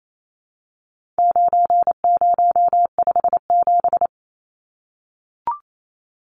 Morse code, heard as a keyed tone, about 28 wpm, 700 Hz. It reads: 9057
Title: Numbers - 4-Digit 28wpm